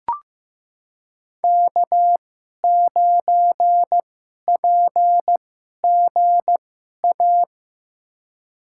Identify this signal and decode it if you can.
Morse code, heard as a keyed tone, about 15 wpm, spaced as for 11 wpm, 700 Hz. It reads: K9PGA
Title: USA Call Signs - 5 Characters - 2x Character Spacing - 15wpm